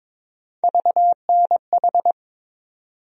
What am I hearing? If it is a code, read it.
Morse code, heard as a keyed tone, about 22 wpm, 700 Hz. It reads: VNH
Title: USA Call Sign Postfixes - 3 Letters 22wpm